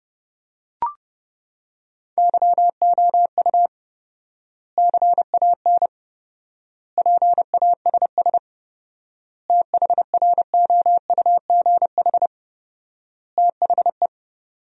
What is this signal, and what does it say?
Morse code, heard as a keyed tone, about 30 wpm, 700 Hz. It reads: YOU CAN PASS THROUGH THE
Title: 4x Word-Spacing - Sentences from Top 400 Words - 30wpm